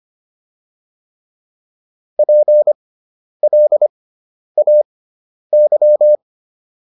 Morse code, heard as a keyed tone, about 25 wpm, spaced as for 10 wpm, 600 Hz. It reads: PLAY